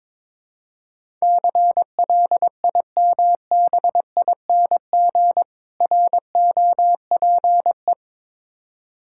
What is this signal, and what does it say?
Morse code, heard as a keyed tone, about 22 wpm, 690 Hz. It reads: CLIMBING ROPE
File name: Difficult Words with Hints - 22wpm